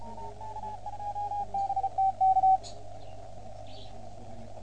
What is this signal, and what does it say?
Morse code, heard as a keyed tone, about 34 wpm, 780 Hz. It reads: BT K